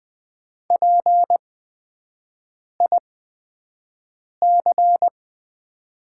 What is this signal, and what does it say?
Morse code, heard as a keyed tone, about 20 wpm, spaced as for 5 wpm, 700 Hz. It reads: PIC
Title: 8x Character Spacing - Top 500 Words - 20wpm